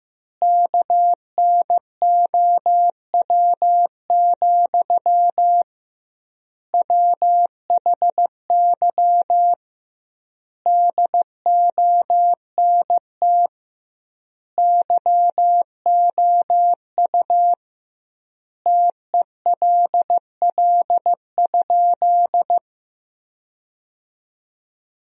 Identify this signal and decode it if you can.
Morse code, heard as a keyed tone, about 16 wpm, 700 Hz. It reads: KNOW, WHY DONT YOU TELL?
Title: 2x Word-Spacing - Sentences from Top 300 Words - 15wpm